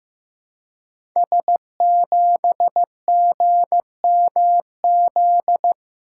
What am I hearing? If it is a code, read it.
Morse code, heard as a keyed tone, about 15 wpm, 700 Hz. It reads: S7GMZ
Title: Code Groups - 5 Letter:Numbers - 15wpm